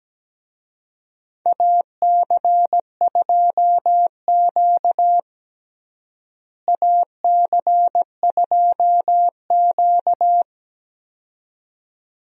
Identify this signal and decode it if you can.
Morse code, heard as a keyed tone, about 17 wpm, 700 Hz. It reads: AC2Q AC2Q
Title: USA Call Signs - 4 Characters - Repeated 2x - 17wpm